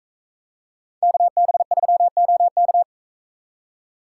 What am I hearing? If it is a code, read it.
Morse code, heard as a keyed tone, about 40 wpm, 700 Hz. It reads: KB3OK